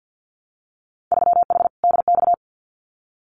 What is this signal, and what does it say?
Morse code, heard as a keyed tone, about 50 wpm, 700 Hz. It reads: 3H DX
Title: Winter Field Day Exchanges Encore 50wpm